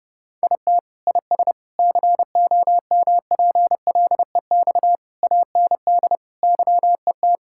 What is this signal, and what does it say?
Morse code, heard as a keyed tone, about 30 wpm, 700 Hz. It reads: IT IS COMPLEX AND YET
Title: Sentences from Top 1000 Words - Review Part 6 - 30wpm